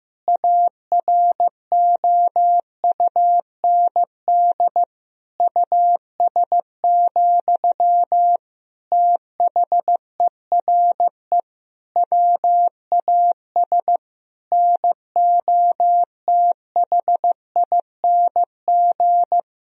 Morse code, heard as a keyed tone, about 16 wpm, 700 Hz. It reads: AROUND US, THERE WAS NOTHING
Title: Sentences from Top 1000 Words - Review Part 3 - 15wpm